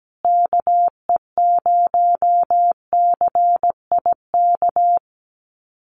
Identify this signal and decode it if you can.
Morse code, heard as a keyed tone, about 17 wpm, 700 Hz. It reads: KE0CIK
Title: USA Call Signs Repeated 3x 17wpm